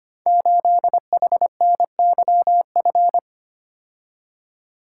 Morse code, heard as a keyed tone, about 25 wpm, 700 Hz. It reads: 8HNYF
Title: Code Groups - 5 Letter:Numbers - 25wpm